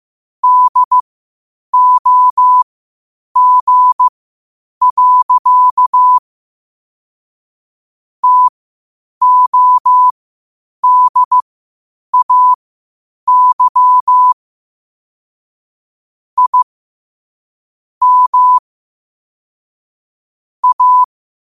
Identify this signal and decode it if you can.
Morse code, heard as a keyed tone, about 15 wpm, spaced as for 8 wpm, 1000 Hz. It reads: DOG. TODAY I M A
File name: qotdfiles / 8wpm